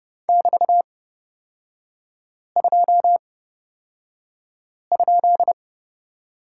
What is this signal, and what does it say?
Morse code, heard as a keyed tone, about 30 wpm, spaced as for 5 wpm, 700 Hz. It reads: <BT>2?
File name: Lesson 36e - BT - Three Characters 30wpm